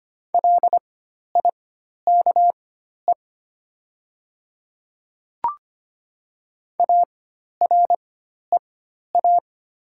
Morse code, heard as a keyed tone, about 25 wpm, spaced as for 12 wpm, 690 Hz. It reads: LIKE AREA
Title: 4x Character Spacing - Top 500 Words - 25wpm